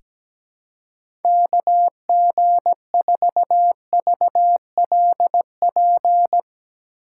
Morse code, heard as a keyed tone, about 17 wpm, 700 Hz. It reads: KG4VLP
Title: USA Call Signs - 6 Characters - 2x Character Spacing - 17wpm